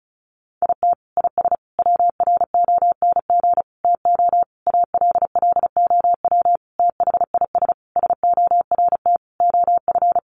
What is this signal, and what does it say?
Morse code, heard as a keyed tone, about 34 wpm, 700 Hz. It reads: IT IS WRONG TO ALLOW THIS SORT OF
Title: Sentences from Top 1300 Words 35wpm